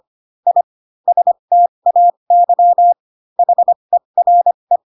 Morse code, heard as a keyed tone, about 25 wpm, spaced as for 22 wpm, 700 Hz. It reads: I STAY HERE
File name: Pinocchio / Pinocchio ch4 18 25 ik1hge